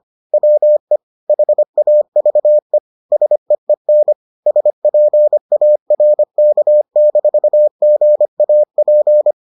Morse code, heard as a keyed tone, about 25 wpm, 600 Hz. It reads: WE HAVE SEEN SPARK-GAP